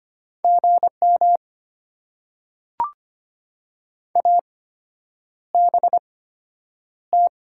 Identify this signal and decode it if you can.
Morse code, heard as a keyed tone, about 25 wpm, spaced as for 9 wpm, 700 Hz. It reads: GM ABT
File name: Top 90 QSO Elements - 8x Character Spacing - 25wpm